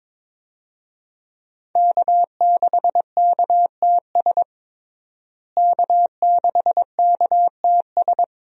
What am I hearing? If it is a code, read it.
Morse code, heard as a keyed tone, about 22 wpm, 700 Hz. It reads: K6KTS K6KTS
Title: USA Call Signs - 5 Characters - Repeated 2x - 22wpm